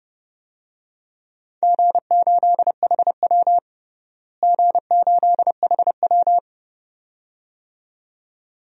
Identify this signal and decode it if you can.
Morse code, heard as a keyed tone, about 32 wpm, 700 Hz. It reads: G8HW G8HW
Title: DXCC-100 Call Signs - 4 Characters - Repeated 2x - 30wpm